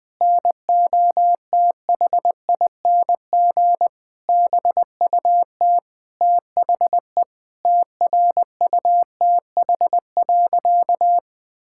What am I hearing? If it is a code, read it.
Morse code, heard as a keyed tone, about 20 wpm, 700 Hz. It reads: NOTHING BUT THE TRUTH.
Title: Sentences with 8-Word Phrases - 20wpm